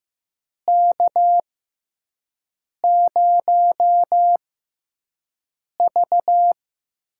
Morse code, heard as a keyed tone, about 15 wpm, spaced as for 5 wpm, 700 Hz. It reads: K0V
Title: USA Call Signs - 4 Characters - 6x Character Spacing - 15wpm